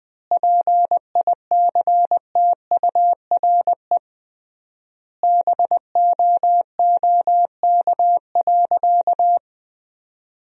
Morse code, heard as a keyed tone, about 20 wpm, 700 Hz. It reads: PICTURE BOOK.
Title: 3x Word-Spacing - Sentences from Top 500 Words - 20wpm